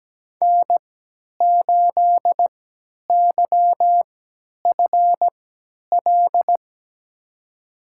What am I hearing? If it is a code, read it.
Morse code, heard as a keyed tone, about 17 wpm, spaced as for 10 wpm, 700 Hz. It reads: N8YFL
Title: USA Call Signs - 5 Characters - 3x Character Spacing - 17wpm